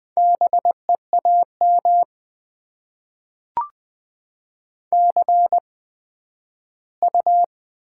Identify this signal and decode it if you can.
Morse code, heard as a keyed tone, about 20 wpm, spaced as for 12 wpm, 700 Hz. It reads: BEAM CU
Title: Top 90 QSO Elements - 8x Character Spacing - 20wpm